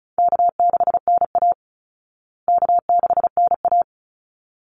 Morse code, heard as a keyed tone, about 34 wpm, 700 Hz. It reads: K6NA K6NA
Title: Easy Call Signs Repeated 2x - 35wpm